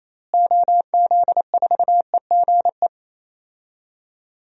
Morse code, heard as a keyed tone, about 28 wpm, 700 Hz. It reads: OZ4EGE
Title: DXCC-100 Call Signs - 6 Characters - Repeated 3x - 28wpm